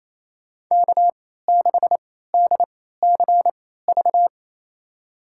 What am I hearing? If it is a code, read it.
Morse code, heard as a keyed tone, about 27 wpm, spaced as for 16 wpm, 700 Hz. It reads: K6DCV
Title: USA Call Signs - 3x Character Spacing - 28wpm